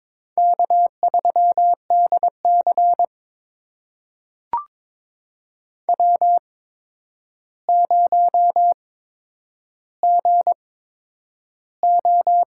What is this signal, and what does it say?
Morse code, heard as a keyed tone, about 22 wpm, spaced as for 9 wpm, 700 Hz. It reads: K3DC W0GO
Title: USA Call Signs - 4 Characters - 8x Character Spacing - 22wpm